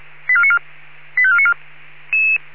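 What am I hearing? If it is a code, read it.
Five-tone selective call (ZVEI), heard as quick runs of five tones, tones of about 70 ms. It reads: 85494
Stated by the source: ZVEI five-tone selective call received on SDR.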